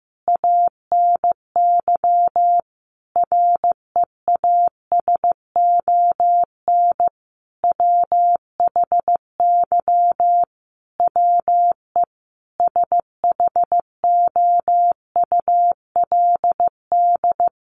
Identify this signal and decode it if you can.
Morse code, heard as a keyed tone, about 15 wpm, 700 Hz. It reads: ANY REASON WHY WE SHOULD
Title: Sentences from Top 1100 Words 15wpm